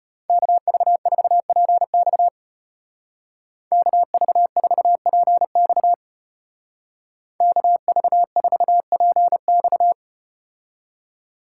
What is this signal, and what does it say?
Morse code, heard as a keyed tone, about 34 wpm, 700 Hz. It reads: KV4PX KV4PX KV4PX